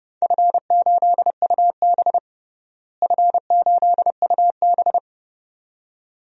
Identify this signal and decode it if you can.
Morse code, heard as a keyed tone, about 30 wpm, 700 Hz. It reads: F8UB F8UB
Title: DXCC-100 Call Signs - 4 Characters - Repeated 2x - 30wpm